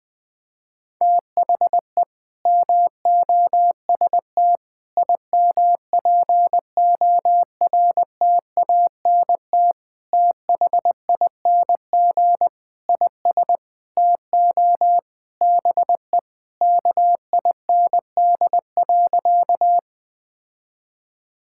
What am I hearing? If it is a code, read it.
Morse code, heard as a keyed tone, about 20 wpm, 700 Hz. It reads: THE MOST IMPORTANT THING IS TO BE KIND.
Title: Sentences with 4-Word Phrases - 20wpm